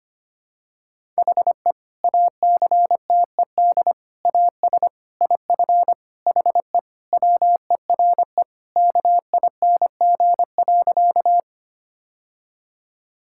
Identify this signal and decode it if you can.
Morse code, heard as a keyed tone, about 25 wpm, 700 Hz. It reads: HE ACTED AS IF HE WERE KING.
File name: practice / 25wpm / Sentences with 4-Word Phrases - 25wpm